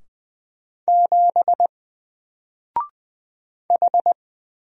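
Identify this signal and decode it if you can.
Morse code, heard as a keyed tone, about 20 wpm, 700 Hz. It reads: 7 H
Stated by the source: Single letter and number Morse drill at 20 wpm, from N4LCD.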